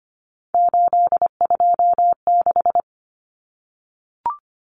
Morse code, heard as a keyed tone, about 25 wpm, 700 Hz. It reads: 826
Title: Common Cut Numbers - 3-Digit 25wpm